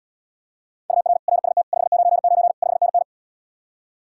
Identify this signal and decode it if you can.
Morse code, heard as a keyed tone, about 77 wpm, 700 Hz. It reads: UR RST 599 5NN